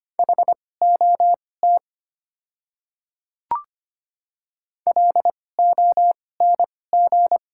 Morse code, heard as a keyed tone, about 25 wpm, spaced as for 18 wpm, 700 Hz. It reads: HOT LONG
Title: 2x Character Spacing - Top 200 Words - 25wpm